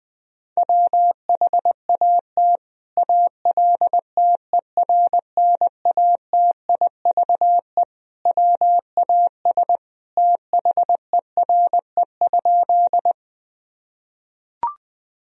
Morse code, heard as a keyed tone, about 20 wpm, 700 Hz. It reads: WHAT ALTERNATIVE WAS THERE?
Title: Sentences from Top 1000 Words - Review Part 4 - 20wpm